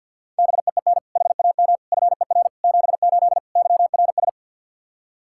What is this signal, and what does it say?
Morse code, heard as a keyed tone, about 51 wpm, 700 Hz. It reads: BEEN HAM FER 78 YRS
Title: K1USN SST QSO Practice 50wpm